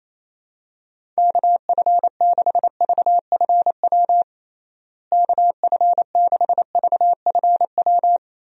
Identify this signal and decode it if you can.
Morse code, heard as a keyed tone, about 28 wpm, 700 Hz. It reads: KF6VFW KF6VFW
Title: USA Call Signs - 6 Characters - Repeated 3x - 28wpm